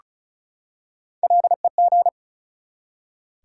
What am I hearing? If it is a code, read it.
Morse code, heard as a keyed tone, about 36 wpm, 700 Hz. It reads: LEG
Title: Three Letter Words - 35wpm